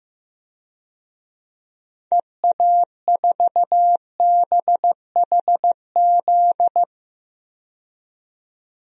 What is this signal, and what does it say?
Morse code, heard as a keyed tone, about 15 wpm, 700 Hz. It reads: EA4BHZ
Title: DXCC-100 Call Signs - 6 Characters - 15wpm